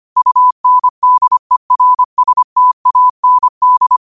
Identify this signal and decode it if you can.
Morse code, heard as a keyed tone, about 25 wpm, 1000 Hz. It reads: UNDERSTAND